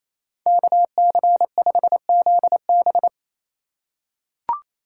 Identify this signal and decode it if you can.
Morse code, heard as a keyed tone, about 28 wpm, 700 Hz. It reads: KC5ZB
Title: USA Call Signs - 5 Characters - 2x Character Spacing - 28wpm